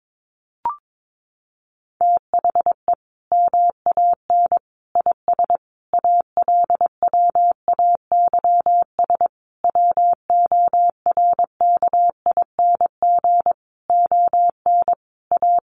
Morse code, heard as a keyed tone, about 22 wpm, 700 Hz. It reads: THE MAN IS ALWAYS WORKING ON A
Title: Sentences from Top 1400 Words 22wpm